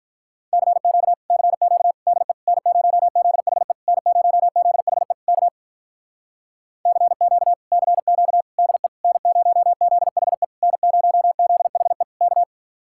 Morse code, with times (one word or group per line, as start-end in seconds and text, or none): 0.5-1.1 CQ
1.3-1.9 CQ
2.1-2.3 DE
2.5-3.7 N0ZHE
3.9-5.1 N0ZHE
5.3-5.5 K
6.9-7.5 CQ
7.7-8.4 CQ
8.6-8.9 DE
9.0-10.4 N0ZHE
10.6-12.0 N0ZHE
12.2-12.4 K